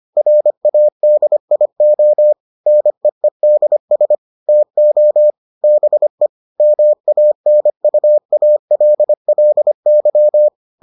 Morse code, heard as a keyed tone, about 25 wpm, 600 Hz. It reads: RADIO NEEDS TO BE MANUALLY